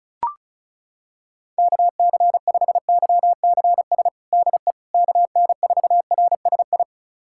Morse code, heard as a keyed tone, about 36 wpm, 690 Hz. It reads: KC5YCS DE KN4RSI
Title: USA Callsign DE Callsign - 6 Characters - 35wpm